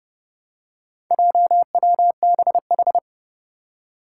Morse code, heard as a keyed tone, about 30 wpm, 700 Hz. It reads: JWBH
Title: Four Challenging Letter-Number Combinations 30wpm